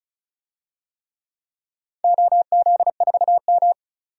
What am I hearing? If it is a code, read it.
Morse code, heard as a keyed tone, about 34 wpm, 700 Hz. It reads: OZ4M
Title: DXCC-100 Call Signs - 4 Characters - 35wpm